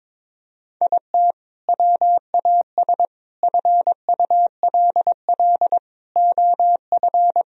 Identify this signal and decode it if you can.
Morse code, heard as a keyed tone, about 22 wpm, 690 Hz. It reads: IT WAS FULL OF
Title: Sentences from Top 1000 Words - Review Part 3 - 22wpm